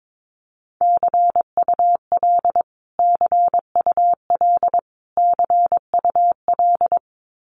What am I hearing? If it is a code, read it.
Morse code, heard as a keyed tone, about 22 wpm, 700 Hz. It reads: CUL CUL CUL
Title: Abbreviations Repeated 3x - 22wpm